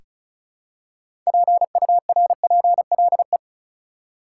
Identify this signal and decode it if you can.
Morse code, heard as a keyed tone, about 35 wpm, 700 Hz. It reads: PURPLE
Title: Six Letter Words - 35wpm